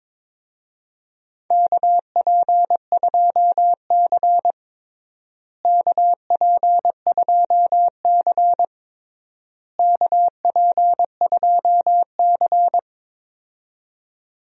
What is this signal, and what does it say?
Morse code, heard as a keyed tone, about 22 wpm, 700 Hz. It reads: KP2C KP2C KP2C
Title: DXCC-100 Call Signs - 4 Characters - Repeated 3x - 22wpm